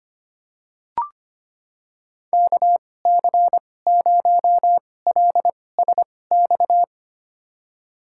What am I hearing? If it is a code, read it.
Morse code, heard as a keyed tone, about 25 wpm, spaced as for 18 wpm, 700 Hz. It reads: KC0LSX
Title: USA Call Signs - 2x Character Spacing - 25wpm